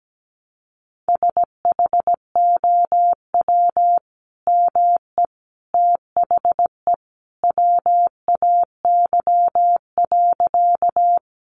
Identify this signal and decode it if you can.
Morse code, heard as a keyed tone, about 17 wpm, 700 Hz. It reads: SHOW ME THE WAY.
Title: Sentences from Top 200 Words 17wpm